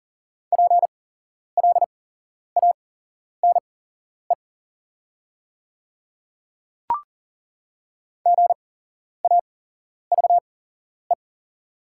Morse code, heard as a keyed tone, about 38 wpm, spaced as for 11 wpm, 690 Hz. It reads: PLANE GAVE
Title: 8x Character Spacing - Top 500 Words - 40wpm